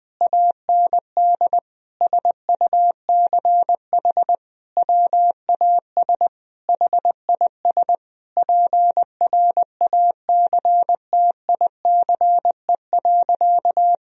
Morse code, heard as a keyed tone, about 21 wpm, 700 Hz. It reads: AND SUCH WAS HIS PRACTICE.
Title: Sentences from Top 1000 Words - Review Part 4 - 20wpm